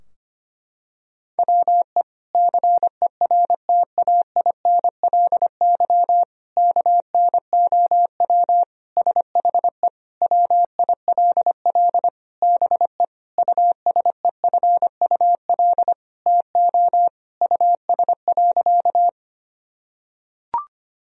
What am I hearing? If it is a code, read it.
Morse code, heard as a keyed tone, about 25 wpm, 700 Hz. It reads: WE CERTAINLY KNOW SHE WILL BE USEFUL TO US.
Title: Sentences from Top 1000 Words - Review Part 4 - 25wpm